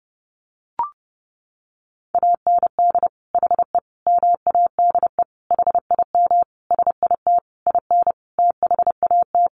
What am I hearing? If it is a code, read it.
Morse code, heard as a keyed tone, about 28 wpm, 700 Hz. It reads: AND HE MADE HIM SIT IN THAT
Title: Sentences from Top 1500 Words 30wpm